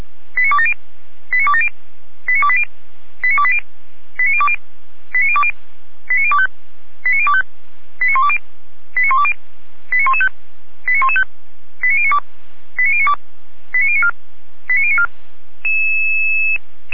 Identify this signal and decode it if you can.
Five-tone selective call (ZVEI), heard as quick runs of five tones, tones of about 70 ms. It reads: 89270 89270 89280 89280 89020 89020 89025 89025 89120 89120 89115 89115 89092 89092 89094 89094